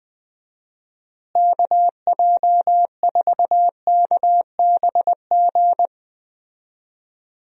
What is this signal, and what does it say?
Morse code, heard as a keyed tone, about 20 wpm, 700 Hz. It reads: KJ4KBG
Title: USA Call Signs - 6 Characters - Repeated 2x - 20wpm